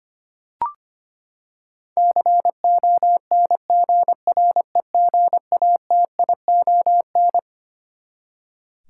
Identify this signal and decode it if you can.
Morse code, heard as a keyed tone, about 25 wpm, 700 Hz. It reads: CONGREGATION
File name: Twelve Letter Words - 25wpm